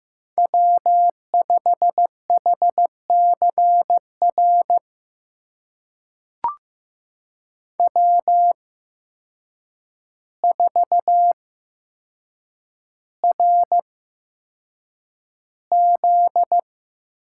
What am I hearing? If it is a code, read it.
Morse code, heard as a keyed tone, about 15 wpm, spaced as for 7 wpm, 700 Hz. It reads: W5HCR W4RZ
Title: USA Call Signs - 5 Characters - 8x Character Spacing - 15wpm